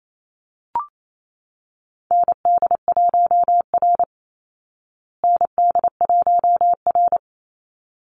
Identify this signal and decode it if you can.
Morse code, heard as a keyed tone, about 28 wpm, 700 Hz. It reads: ND1R ND1R
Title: Call Signs Repeated 3x - 28wpm